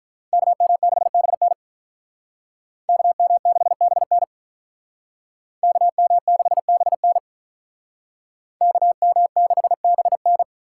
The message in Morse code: KM6BN KM6BN KM6BN KM6BN

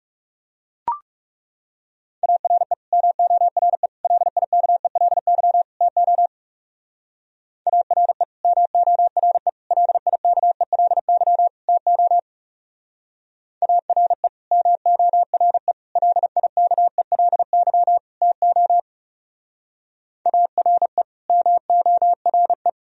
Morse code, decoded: ARE MORE LIKELY TO ARE MORE LIKELY TO ARE MORE LIKELY TO ARE MORE

